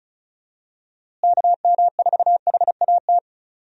Morse code, heard as a keyed tone, about 36 wpm, 700 Hz. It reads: KM4HAT
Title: USA Call Signs - 6 Characters - Repeated 2x - 35wpm